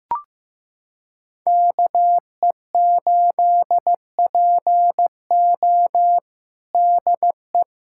Morse code, heard as a keyed tone, about 15 wpm, 700 Hz. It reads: KE8PO DE
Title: USA Callsign DE Callsign - 5 Characters - 15wpm